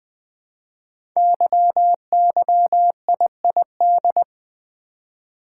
Morse code, heard as a keyed tone, about 21 wpm, 700 Hz. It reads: YYIID
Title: Code Groups - 5 Letter:Numbers - 20wpm